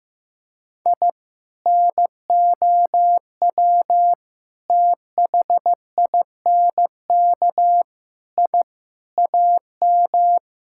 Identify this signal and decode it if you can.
Morse code, heard as a keyed tone, about 16 wpm, 700 Hz. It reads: I NOW THINK I AM
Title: Sentences from Top 1000 Words - Review Part 5 - 15wpm